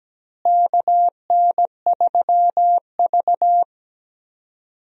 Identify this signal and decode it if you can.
Morse code, heard as a keyed tone, about 17 wpm, 700 Hz. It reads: KN3V
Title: USA Call Signs - 4 Characters - Repeated 3x - 17wpm